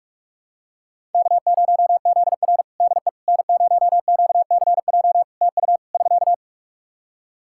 Morse code, heard as a keyed tone, about 46 wpm, 700 Hz. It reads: K0ZR DE N0QCJ TU <SK>
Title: Practice QSO - Exchange 4 - Ending - 45wpm